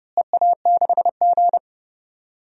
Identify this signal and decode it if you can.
Morse code, heard as a keyed tone, about 30 wpm, 700 Hz. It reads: EA6G
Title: DXCC-100 Call Signs - 4 Characters - Repeated 2x - 30wpm